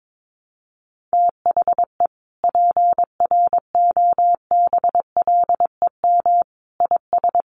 Morse code, heard as a keyed tone, about 22 wpm, 700 Hz. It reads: THE PROBLEM IS
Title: Sets of 5 Words Encore2 22wpm